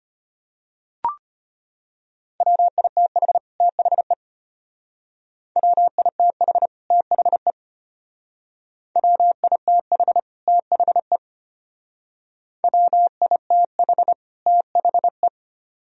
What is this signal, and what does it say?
Morse code, heard as a keyed tone, about 33 wpm, 700 Hz. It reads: WITH THE WITH THE WITH THE WITH THE